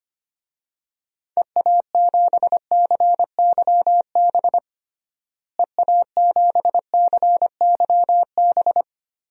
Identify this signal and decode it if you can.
Morse code, heard as a keyed tone, about 25 wpm, 700 Hz. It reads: EA7CYB EA7CYB
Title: DXCC-100 Call Signs - 6 Characters - Repeated 3x - 25wpm